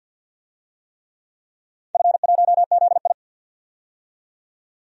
Morse code, heard as a keyed tone, about 47 wpm, 700 Hz. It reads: F1ZI
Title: DXCC-100 Call Signs - 4 Characters - 50wpm